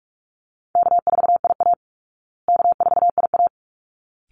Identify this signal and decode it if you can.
Morse code, heard as a keyed tone, about 44 wpm, 700 Hz. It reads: K4IA K4IA
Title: Easy Call Signs Repeated 2x - 45wpm